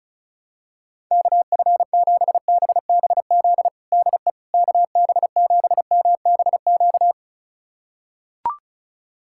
Morse code, heard as a keyed tone, about 34 wpm, 700 Hz. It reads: KF7BBZ DE KB7MBQ
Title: USA Callsign DE Callsign - 6 Characters - 35wpm